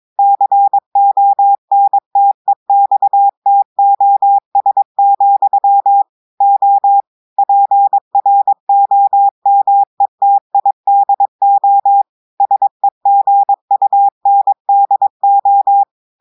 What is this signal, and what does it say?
Morse code, heard as a keyed tone, about 22 wpm, 800 Hz. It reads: CONTEXTOS, O PROMETIDO SEGUNDO
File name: audios / portugais / portugais texte en clair 1 vitesse 22 mots minutes